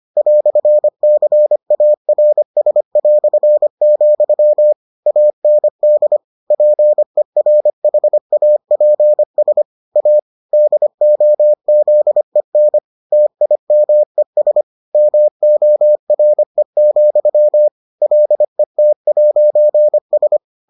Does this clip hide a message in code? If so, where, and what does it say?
Yes: Morse code from 0.2 to 20.4 s, heard as a keyed tone, about 25 wpm, 600 Hz. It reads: "CARS", AND PERHAPS A DOZEN TIMES MORE, LET'S